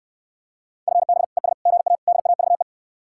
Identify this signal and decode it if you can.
Morse code, heard as a keyed tone, about 69 wpm, 700 Hz. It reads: FB ES GA DAVE